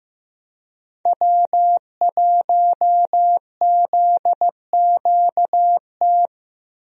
Morse code, heard as a keyed tone, about 15 wpm, 700 Hz. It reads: W1ZQT
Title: USA Call Signs - 5 Characters - Repeated 2x - 15wpm